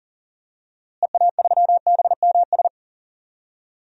Morse code, heard as a keyed tone, about 40 wpm, 700 Hz. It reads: EA3BMS
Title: DXCC-100 Call Signs - 6 Characters - Repeated 2x - 40wpm